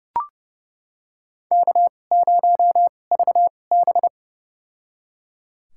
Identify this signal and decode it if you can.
Morse code, heard as a keyed tone, about 30 wpm, spaced as for 22 wpm, 700 Hz. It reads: K0VB
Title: USA Call Signs - 2x Character Spacing - 30wpm